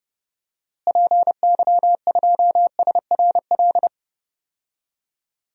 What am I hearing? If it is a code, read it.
Morse code, heard as a keyed tone, about 30 wpm, 700 Hz. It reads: PY2SRL
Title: DXCC-100 Call Signs - 6 Characters - Repeated 2x - 30wpm